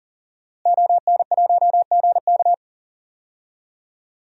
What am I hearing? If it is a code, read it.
Morse code, heard as a keyed tone, about 40 wpm, 700 Hz. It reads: ON1GK
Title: DXCC-100 Call Signs - 5 Characters - Repeated 2x - 40wpm